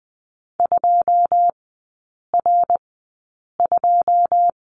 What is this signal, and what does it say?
Morse code, heard as a keyed tone, about 20 wpm, spaced as for 8 wpm, 690 Hz. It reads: 2R2
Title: 081 Lesson 14e - Number 2 - Three Characters 20wpm